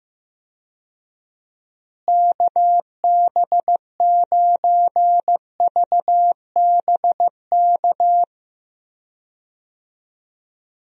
Morse code, heard as a keyed tone, about 15 wpm, 700 Hz. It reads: KB9VBK
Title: USA Call Signs - 6 Characters - Repeated 2x - 15wpm